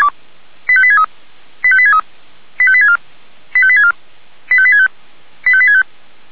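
Five-tone selective call (ZVEI), heard as quick runs of five tones, tones of about 70 ms. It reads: 85863 85863 85864 85864 85865 85865
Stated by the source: Received on SDR as ZVEI five-tone selective call.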